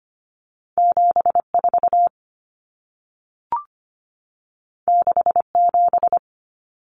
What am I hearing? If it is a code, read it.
Morse code, heard as a keyed tone, about 25 wpm, 700 Hz. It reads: 74 67
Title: Numbers - 2-Digit 25wpm